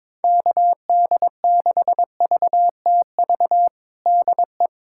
Morse code, heard as a keyed tone, about 22 wpm, 700 Hz. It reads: KD6VTV DE
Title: USA Callsign DE Callsign 22wpm